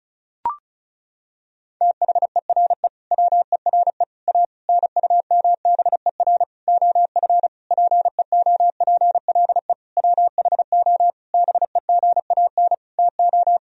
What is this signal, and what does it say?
Morse code, heard as a keyed tone, about 36 wpm, 700 Hz. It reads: THERE WERE A NUMBER OF PEOPLE WHO BEGAN TO
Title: Sentences from Top 1000 Words - Review Part 1 - 35wpm